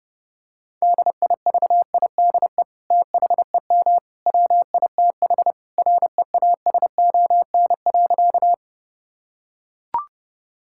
Morse code, heard as a keyed tone, about 29 wpm, 700 Hz. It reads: DIVIDE THEM WITH REASON.
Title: Sentences from Top 1600 Words 30wpm